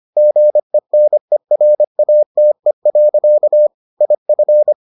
Morse code, heard as a keyed tone, about 25 wpm, 600 Hz. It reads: GENERATE. IF